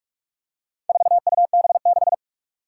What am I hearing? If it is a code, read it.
Morse code, heard as a keyed tone, about 44 wpm, 700 Hz. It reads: 4UB6
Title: Four Challenging Letter-Number Combinations 45wpm